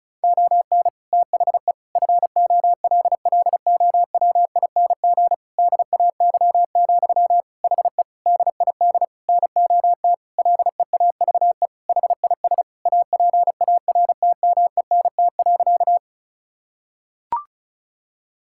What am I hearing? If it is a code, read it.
Morse code, heard as a keyed tone, about 36 wpm, 700 Hz. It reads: ON THE FOLLOWING DAY, HE DID NOT LEAVE HIS APARTMENT.
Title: Sentences from Top 1000 Words - Review Part 6 - 35wpm